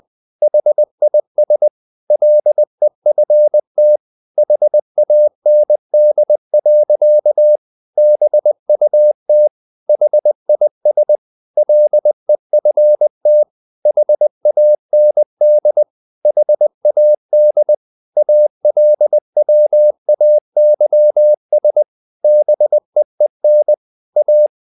Morse code, heard as a keyed tone, about 20 wpm, 600 Hz. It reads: HIS LEFT HAND. BUT HIS LEFT HAND HAD ALWAYS BEEN A
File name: the old man and the sea part 2 20wpm